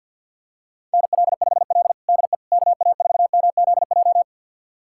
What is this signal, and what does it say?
Morse code, heard as a keyed tone, about 52 wpm, 700 Hz. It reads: NL5L DE KA4MZJ
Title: USA Callsign DE Callsign 50wpm